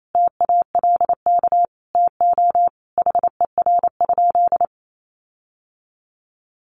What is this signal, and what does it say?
Morse code, heard as a keyed tone, about 28 wpm, 700 Hz. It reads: TALK TO HER?
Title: Sentences from Top 1400 Words 28wpm